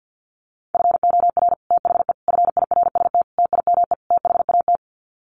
Morse code, heard as a keyed tone, about 50 wpm, 700 Hz. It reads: FOR THE FIRST TIME THAT
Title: Sets of 5 Words Encore 50wpm